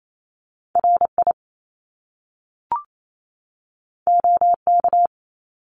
Morse code, heard as a keyed tone, about 28 wpm, 700 Hz. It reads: RI OK
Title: Top Two Letter Combinations in Words 28wpm